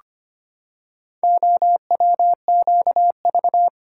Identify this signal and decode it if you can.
Morse code, heard as a keyed tone, about 25 wpm, 700 Hz. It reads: OWQV
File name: practice / 25wpm / Four Letter-Number Combinations - 25wpm